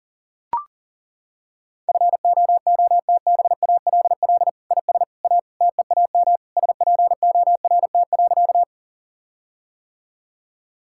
Morse code, heard as a keyed tone, about 40 wpm, 700 Hz. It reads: FOOTBALL IS A TEAM SPORT.
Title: Sentences from Top 1600 Words 40wpm